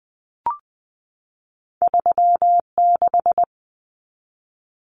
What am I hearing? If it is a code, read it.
Morse code, heard as a keyed tone, about 20 wpm, 700 Hz. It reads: 36
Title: Common Cut Numbers - 2-Digit 20wpm